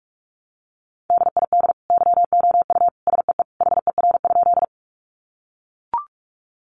Morse code, heard as a keyed tone, about 44 wpm, 700 Hz. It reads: DID YOU SEE HER?
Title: Sentences from Top 100 Words 45wpm